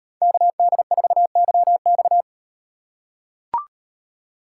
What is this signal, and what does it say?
Morse code, heard as a keyed tone, about 38 wpm, 700 Hz. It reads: KD4YX